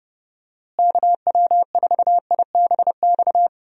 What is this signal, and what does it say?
Morse code, heard as a keyed tone, about 32 wpm, 700 Hz. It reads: KW4IBX
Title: USA Call Signs - 6 Characters - Repeated 3x - 30wpm